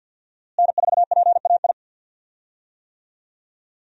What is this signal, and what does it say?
Morse code, heard as a keyed tone, about 49 wpm, 700 Hz. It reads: N4PAI
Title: USA Call Signs - 5 Characters - Repeated 2x - 50wpm